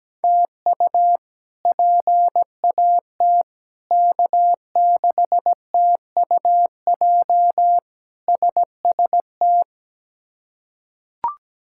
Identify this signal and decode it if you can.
Morse code, heard as a keyed tone, about 17 wpm, 700 Hz. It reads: TU PAT K6TUJ SST
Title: K1USN SST QSO Practice - 17wpm